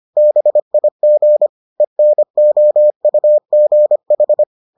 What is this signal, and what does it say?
Morse code, heard as a keyed tone, about 25 wpm, 600 Hz. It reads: BIG ENOUGH